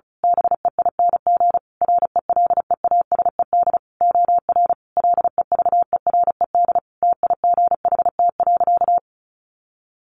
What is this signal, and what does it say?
Morse code, heard as a keyed tone, about 34 wpm, 700 Hz. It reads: BEING RELEASED OR LEVERED TIGHT.
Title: A Sound Of Thunder 35wpm